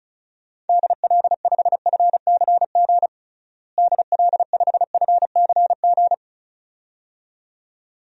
Morse code, heard as a keyed tone, about 36 wpm, 700 Hz. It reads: DL5FCG DL5FCG
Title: DXCC-100 Call Signs - 6 Characters - Repeated 2x - 35wpm